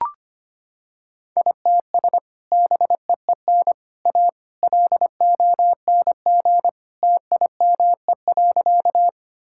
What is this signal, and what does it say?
Morse code, heard as a keyed tone, about 25 wpm, 700 Hz. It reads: ITS BEEN A LONG TIME.
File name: practice / 25wpm / Sentences with 3-Word Phrases - 25wpm